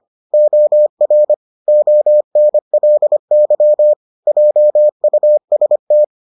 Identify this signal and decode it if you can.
Morse code, heard as a keyed tone, about 25 wpm, 600 Hz. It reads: OR ONLY JUST